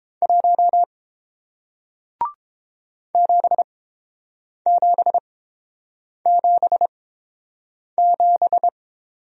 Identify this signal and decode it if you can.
Morse code, heard as a keyed tone, about 30 wpm, spaced as for 7 wpm, 700 Hz. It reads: 1 7777